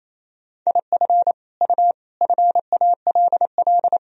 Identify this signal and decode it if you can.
Morse code, heard as a keyed tone, about 28 wpm, 700 Hz. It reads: IF U FALL
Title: Sentences from Top 500 Words and Abbreviations 28wpm